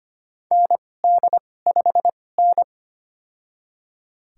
Morse code, heard as a keyed tone, about 25 wpm, spaced as for 18 wpm, 700 Hz. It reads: ND5N
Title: USA Call Signs - 2x Character Spacing - 25wpm